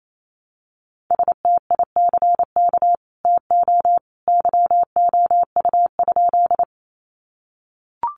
Morse code, heard as a keyed tone, about 28 wpm, 700 Hz. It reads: STICK TO YOU?
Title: Sentences from Top 1200 Words 28wpm